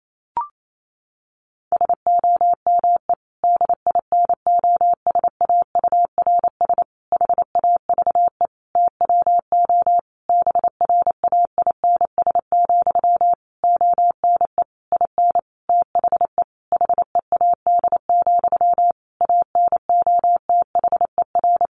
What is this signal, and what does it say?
Morse code, heard as a keyed tone, about 28 wpm, 700 Hz. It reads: SOME DINOSAURS HAVE TWO BRAINS, ONE IN THE HEAD, ANOTHER
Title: A Sound Of Thunder 28wpm